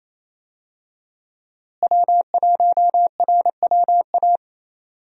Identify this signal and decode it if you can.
Morse code, heard as a keyed tone, about 28 wpm, 700 Hz. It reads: W1RWA
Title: USA Call Signs - 5 Characters - Repeated 2x - 28wpm